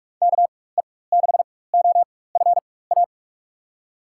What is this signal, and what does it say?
Morse code, heard as a keyed tone, about 42 wpm, spaced as for 20 wpm, 700 Hz. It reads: KE6OFA
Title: USA Call Signs - 6 Characters - 4x Character Spacing - 45wpm